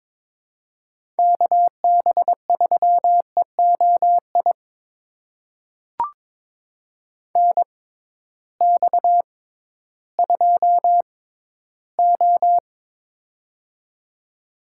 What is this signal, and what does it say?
Morse code, heard as a keyed tone, about 22 wpm, spaced as for 13 wpm, 700 Hz. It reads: KB3EOI NX2O
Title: USA Call Signs - 6x Character Spacing - 22wpm